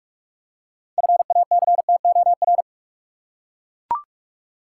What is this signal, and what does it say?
Morse code, heard as a keyed tone, about 41 wpm, 700 Hz. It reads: FACTOR
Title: Six Letter Words - 45wpm